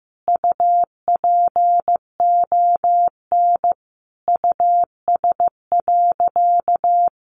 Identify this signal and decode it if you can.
Morse code, heard as a keyed tone, about 15 wpm, 700 Hz. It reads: UPON US.
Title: Sentences from Top 1400 Words 15wpm